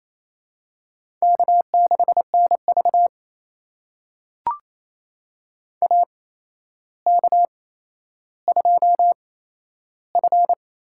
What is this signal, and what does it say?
Morse code, heard as a keyed tone, about 28 wpm, spaced as for 12 wpm, 700 Hz. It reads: K6NV AK2F
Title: USA Call Signs - 4 Characters - 8x Character Spacing - 28wpm